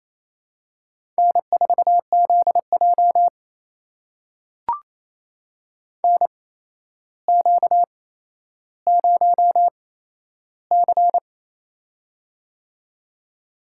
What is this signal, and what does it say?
Morse code, heard as a keyed tone, about 28 wpm, spaced as for 12 wpm, 700 Hz. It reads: N4ZJ NQ0C
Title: USA Call Signs - 4 Characters - 8x Character Spacing - 28wpm